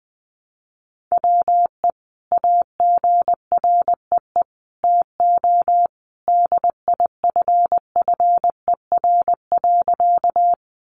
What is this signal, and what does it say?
Morse code, heard as a keyed tone, about 20 wpm, 700 Hz. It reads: WE AGREE TO DIFFER.
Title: Sentences from Top 1100 Words 20wpm